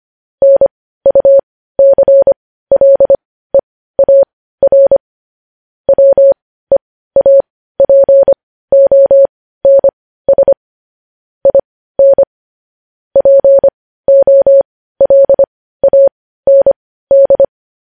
Morse code, heard as a keyed tone, about 25 wpm, spaced as for 15 wpm, 550 Hz. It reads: NUCLEAR WEAPONS IN POLAND
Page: news headlines in morse code at 15 wpm